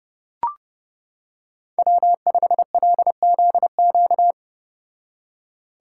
Morse code, heard as a keyed tone, about 30 wpm, 700 Hz. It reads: W5LZQ